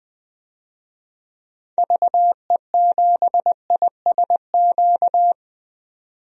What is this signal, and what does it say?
Morse code, heard as a keyed tone, about 20 wpm, 700 Hz. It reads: VE7ISQ
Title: DXCC-100 Call Signs - 6 Characters - 20wpm